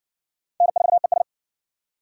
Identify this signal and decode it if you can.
Morse code, heard as a keyed tone, about 59 wpm, 700 Hz. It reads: N4ES